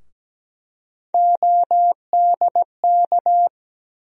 Morse code, heard as a keyed tone, about 17 wpm, 700 Hz. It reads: ODK
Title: USA Call Sign Postfixes - 3 Letters 17wpm